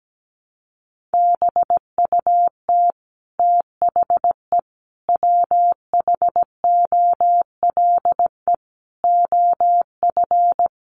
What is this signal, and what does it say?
Morse code, heard as a keyed tone, about 17 wpm, 700 Hz. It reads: BUT THE WHOLE OF
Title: Sentences from Top 1500 Words 17wpm